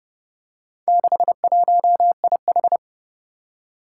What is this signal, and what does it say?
Morse code, heard as a keyed tone, about 30 wpm, 700 Hz. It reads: 61IH
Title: Four Challenging Letter-Number Combinations 30wpm